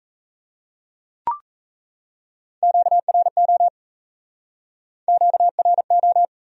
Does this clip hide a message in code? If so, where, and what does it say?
Yes: Morse code from 2.6 to 6.2 s, heard as a keyed tone, about 39 wpm, 700 Hz. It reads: QRO QRO